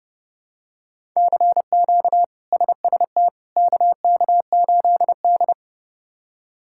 Morse code, heard as a keyed tone, about 32 wpm, 700 Hz. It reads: CQ SST KK8D
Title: K1USN SST QSO Practice - 30wpm